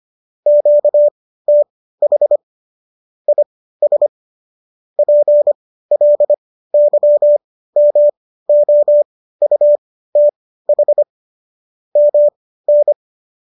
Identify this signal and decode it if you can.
Morse code, heard as a keyed tone, about 25 wpm, spaced as for 15 wpm, 590 Hz. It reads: QTH IS PLYMOUTH MN